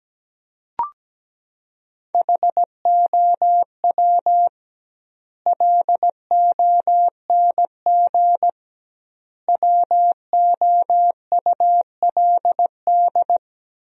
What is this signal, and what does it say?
Morse code, heard as a keyed tone, about 17 wpm, 700 Hz. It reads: HOW LONG WOULD
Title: 2x Word-Spacing - Sentences from Top 500 Words - 17wpm